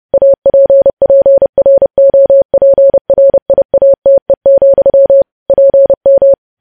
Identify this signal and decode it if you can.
Morse code, heard as a keyed tone, about 30 wpm, 550 Hz. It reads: APPROPRIATE, PM
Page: news headlines in morse code at 30 wpm